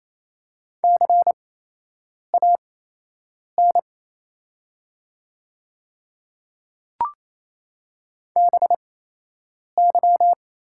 Morse code, heard as a keyed tone, about 28 wpm, spaced as for 8 wpm, 700 Hz. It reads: CAN BY
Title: 8x Character Spacing - Top 100 Words - 28wpm